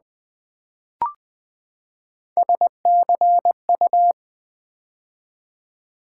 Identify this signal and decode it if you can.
Morse code, heard as a keyed tone, about 20 wpm, 700 Hz. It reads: SCU
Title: Build A Word - 3 Letters At A Time - 9-Letter Words - 20wpm